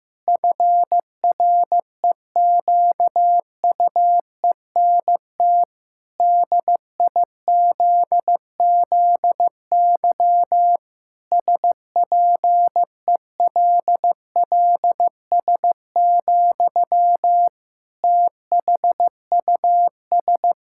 Morse code, heard as a keyed tone, about 15 wpm, 700 Hz. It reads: FREQUENT DIZZY SPELLS, THUS
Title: Pangram Challenge v2 - 15wpm